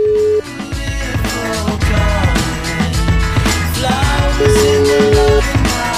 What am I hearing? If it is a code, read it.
Ringback tone (Europe, 425 Hz), with noise.